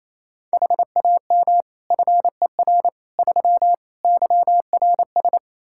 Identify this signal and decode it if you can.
Morse code, heard as a keyed tone, about 28 wpm, 700 Hz. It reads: HAM FER 3 YRS
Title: Practice QSO - Exchange 3 - More Info - 28wpm